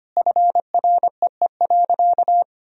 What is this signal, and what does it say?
Morse code, heard as a keyed tone, about 25 wpm, 700 Hz. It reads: FREE.
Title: Sentences from Top 1000 Words - Review Part 1 - 25wpm